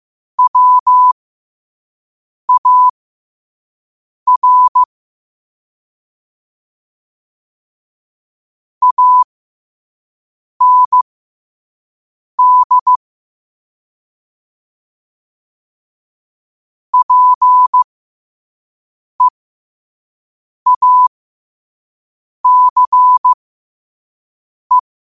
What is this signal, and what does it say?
Morse code, heard as a keyed tone, about 15 wpm, spaced as for 5 wpm, 1000 Hz. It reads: WAR AND PEACE
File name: qotdfiles / 5wpm